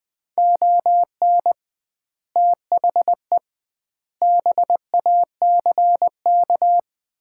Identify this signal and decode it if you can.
Morse code, heard as a keyed tone, about 20 wpm, 700 Hz. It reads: ON THE BACK
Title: 2x Word-Spacing - Sets of 5 Words - 20wpm